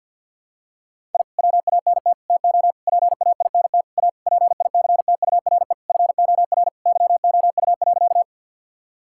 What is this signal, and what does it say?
Morse code, heard as a keyed tone, about 51 wpm, 700 Hz. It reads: I WANT TO PAINT A PICTURE FOR YOU.
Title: Sentences from Top 1000 Words - Review Part 6 - 50wpm